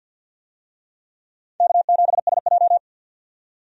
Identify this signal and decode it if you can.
Morse code, heard as a keyed tone, about 51 wpm, 700 Hz. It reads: K7SJ
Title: USA Call Signs - 4 Characters - 50wpm